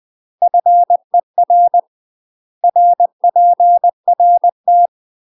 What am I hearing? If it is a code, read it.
Morse code, heard as a keyed tone, about 20 wpm, 700 Hz. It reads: FER RPRT
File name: UK MASTER QSOs TRAINING/20WPM